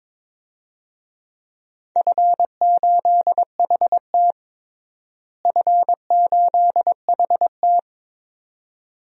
Morse code, heard as a keyed tone, about 22 wpm, 700 Hz. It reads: F8HT F8HT
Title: DXCC-100 Call Signs - 4 Characters - Repeated 2x - 22wpm